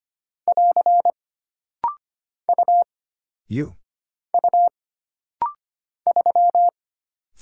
Morse code, heard as a keyed tone, about 25 wpm, 700 Hz. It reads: <AR> U U 3
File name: Single Character (Letter:Number:Punctuation:Prosign) - Rapid-Fire - 25wpm